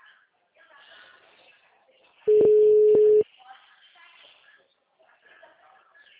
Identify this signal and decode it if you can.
Ringback tone (Europe, 425 Hz).